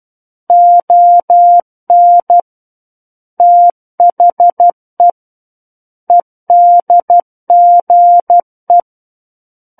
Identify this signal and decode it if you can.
Morse code, heard as a keyed tone, about 12 wpm, 700 Hz. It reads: ON THE EDGE
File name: Book One / 03-Chapter 3 - 12 WPM